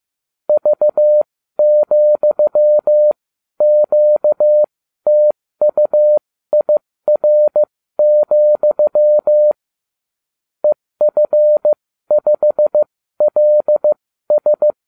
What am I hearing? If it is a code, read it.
Morse code, heard as a keyed tone, about 15 wpm, spaced as for 12 wpm, 600 Hz. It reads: V,QTUIR, EF5LS